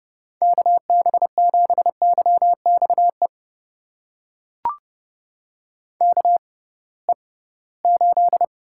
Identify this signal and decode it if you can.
Morse code, heard as a keyed tone, about 30 wpm, spaced as for 19 wpm, 700 Hz. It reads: KB7YXE KE8
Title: USA Call Signs - 6 Characters - 6x Character Spacing - 30wpm